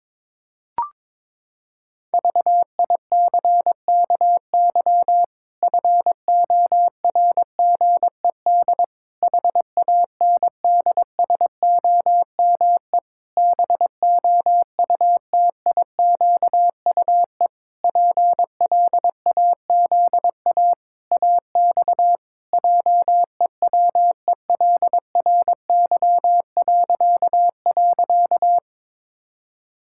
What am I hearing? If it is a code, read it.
Morse code, heard as a keyed tone, about 22 wpm, 700 Hz. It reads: VICKY FORGED HANDSOME BOUTIQUE PLAZA AX JEWELRY..
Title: Pangram Challenge v2 - 22wpm